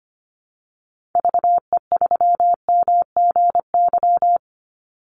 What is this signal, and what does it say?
Morse code, heard as a keyed tone, about 25 wpm, 700 Hz. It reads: VE3MGY
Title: Call Signs Repeated 3x - 25wpm